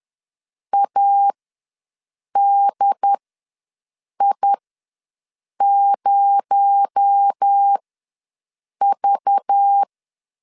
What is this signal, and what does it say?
Morse code, heard as a keyed tone, about 10 wpm, spaced as for 6 wpm, 800 Hz. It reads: ADI0V
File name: LEZIONI / 8 YJ45 / 8 Y ws 10wpm 3952